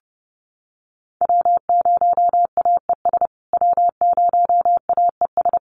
Morse code, heard as a keyed tone, about 30 wpm, 700 Hz. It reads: W0AES W0AES
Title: USA Call Signs Repeated 3x 30wpm